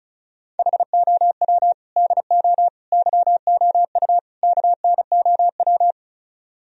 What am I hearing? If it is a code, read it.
Morse code, heard as a keyed tone, about 36 wpm, 700 Hz. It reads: HOW DO YOU KNOW
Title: Speed-Racing - Sets of 4 Words 30wpm